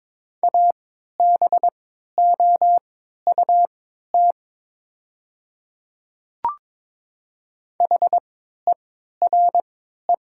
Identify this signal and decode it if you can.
Morse code, heard as a keyed tone, about 22 wpm, spaced as for 12 wpm, 700 Hz. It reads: ABOUT HERE
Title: 3x Character Spacing - Top 500 Words - 22wpm